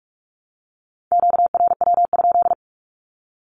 Morse code, heard as a keyed tone, about 44 wpm, 700 Hz. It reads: QRW?
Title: Q-Codes Repeated 3x 45wpm